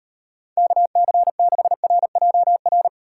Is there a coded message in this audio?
Yes: Morse code from 0.6 to 2.9 s, heard as a keyed tone, about 36 wpm, 700 Hz. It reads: KC6RJR